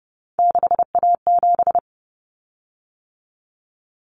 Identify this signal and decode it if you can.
Morse code, heard as a keyed tone, about 30 wpm, 700 Hz. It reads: 6A7
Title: Common Cut Numbers - 3-Digit 30wpm